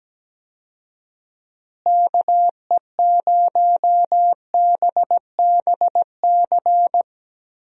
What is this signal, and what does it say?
Morse code, heard as a keyed tone, about 17 wpm, 700 Hz. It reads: KE0BBC
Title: USA Call Signs - 6 Characters - Repeated 2x - 17wpm